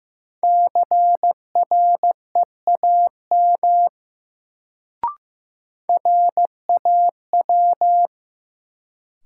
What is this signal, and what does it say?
Morse code, heard as a keyed tone, about 15 wpm, 700 Hz. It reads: CREAM RAW
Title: Sentences from Top 1500 Words - New Word Drill - 15wpm